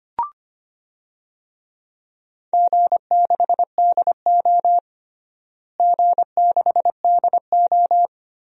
Morse code, heard as a keyed tone, about 25 wpm, 700 Hz. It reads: G6DO G6DO
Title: DXCC-100 Call Signs - 4 Characters - Repeated 2x - 25wpm